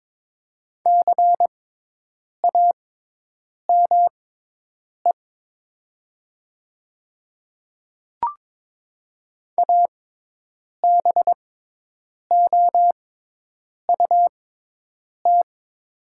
Morse code, heard as a keyed tone, about 22 wpm, spaced as for 8 wpm, 700 Hz. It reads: CAME ABOUT
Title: 6x Character Spacing - Top 200 Words - 22wpm